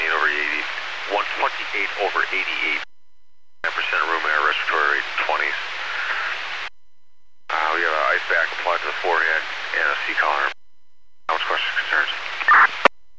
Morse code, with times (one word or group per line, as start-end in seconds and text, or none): none